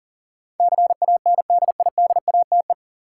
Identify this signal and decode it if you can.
Morse code, heard as a keyed tone, about 37 wpm, 700 Hz. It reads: CANDIDATE
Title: Sentences from Top 1600 Words - New Word Drill - 40wpm